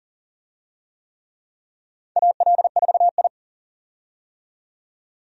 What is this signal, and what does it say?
Morse code, heard as a keyed tone, about 43 wpm, 690 Hz. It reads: AL4I
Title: USA Call Signs - 4 Characters - 40wpm